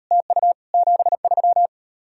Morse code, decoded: TU 73

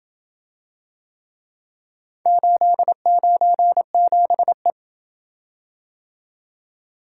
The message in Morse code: 897E

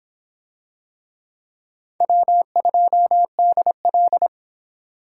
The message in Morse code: W2DL

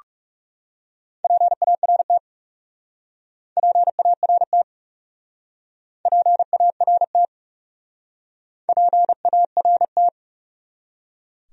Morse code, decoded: PART PART PART PART

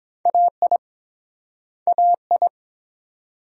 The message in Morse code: AI AI

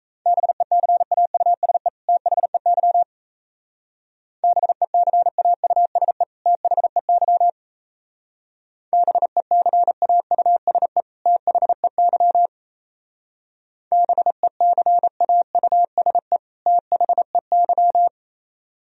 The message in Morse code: BECAUSE THEY BECAUSE THEY BECAUSE THEY BECAUSE THEY

